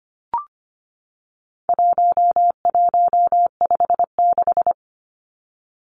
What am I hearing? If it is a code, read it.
Morse code, heard as a keyed tone, about 25 wpm, 700 Hz. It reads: 1156
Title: Numbers - 4-Digit 25wpm